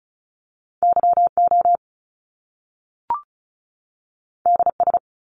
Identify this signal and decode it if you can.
Morse code, heard as a keyed tone, about 36 wpm, 700 Hz. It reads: Y O D S
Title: Top Two Letter Combinations in Words 35wpm